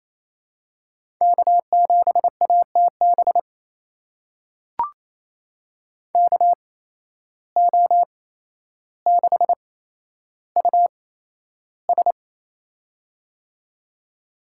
Morse code, heard as a keyed tone, about 28 wpm, spaced as for 12 wpm, 700 Hz. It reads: K7ATB KO6US
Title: USA Call Signs - 5 Characters - 8x Character Spacing - 28wpm